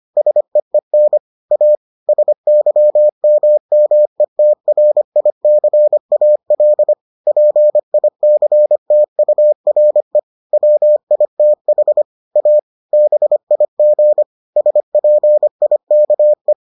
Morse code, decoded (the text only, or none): SEEN A SYMMETRICAL PICTURE WITH A BIG SPIKE